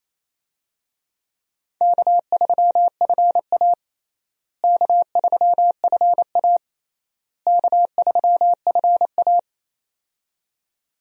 Morse code, heard as a keyed tone, about 28 wpm, 700 Hz. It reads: K3FA K3FA K3FA
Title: USA Call Signs - 4 Characters - Repeated 3x - 28wpm